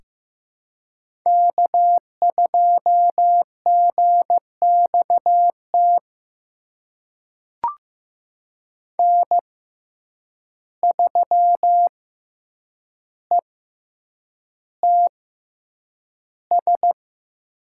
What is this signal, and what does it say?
Morse code, heard as a keyed tone, about 15 wpm, spaced as for 8 wpm, 700 Hz. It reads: K2GXT N3ETS
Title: USA Call Signs - 5 Characters - 6x Character Spacing - 15wpm